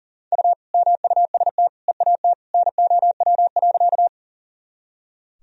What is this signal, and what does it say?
Morse code, heard as a keyed tone, about 38 wpm, 700 Hz. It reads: U MUST EAT NOW.
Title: Sentences from Top 300 Words and Abbreviations 40wpm